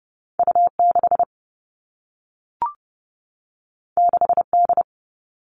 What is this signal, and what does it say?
Morse code, heard as a keyed tone, about 30 wpm, 700 Hz. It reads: U6 6D
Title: All Cut Numbers - 2-Digit 30wpm